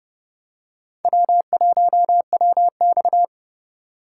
Morse code, heard as a keyed tone, about 30 wpm, 700 Hz. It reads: W1WX
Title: USA Call Signs - 4 Characters - 4x Character Spacing - 30wpm